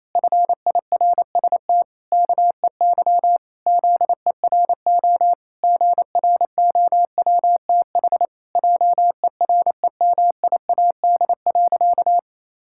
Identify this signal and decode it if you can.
Morse code, heard as a keyed tone, about 27 wpm, 700 Hz. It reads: FIRST KEY ZERO GROWTH JEREMIAD.
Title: Pangram Challenge v2 - 28wpm